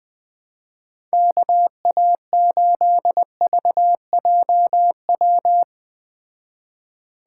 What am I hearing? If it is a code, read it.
Morse code, heard as a keyed tone, about 20 wpm, 700 Hz. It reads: KA8VJW